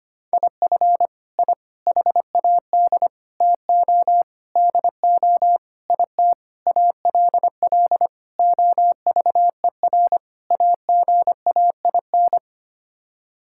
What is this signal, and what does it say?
Morse code, heard as a keyed tone, about 25 wpm, 700 Hz. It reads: IF I HAD TO DO IT ALL OVER AGAIN
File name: Sets of 9 Words - 25wpm